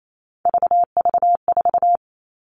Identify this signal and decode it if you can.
Morse code, heard as a keyed tone, about 28 wpm, 700 Hz. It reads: VV4
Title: All Cut Numbers - 3-Digit 28wpm